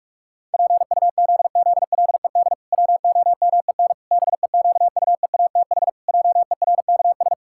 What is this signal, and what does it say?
Morse code, heard as a keyed tone, about 46 wpm, 700 Hz. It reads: PUZZLED WOMEN BEQUEATH JERKS
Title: Pangram Challenge v2 - 45wpm